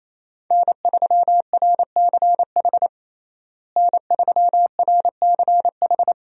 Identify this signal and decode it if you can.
Morse code, heard as a keyed tone, about 28 wpm, 700 Hz. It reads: N3RCH N3RCH
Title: USA Call Signs - 5 Characters - Repeated 3x - 28wpm